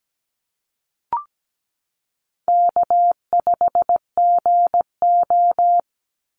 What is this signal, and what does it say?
Morse code, heard as a keyed tone, about 17 wpm, 700 Hz. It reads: K5GO
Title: Call Signs Repeated 3x - 17wpm